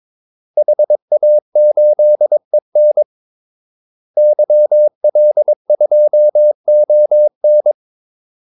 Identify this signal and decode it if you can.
Morse code, heard as a keyed tone, about 22 wpm, 600 Hz. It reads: HA8EN YL2ON
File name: Callsigns-22wpm